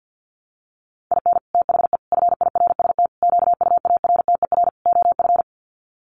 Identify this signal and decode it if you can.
Morse code, heard as a keyed tone, about 50 wpm, 700 Hz. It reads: IN THE FIRST QUARTER OF
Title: Sets of 5 Words Encore2 50wpm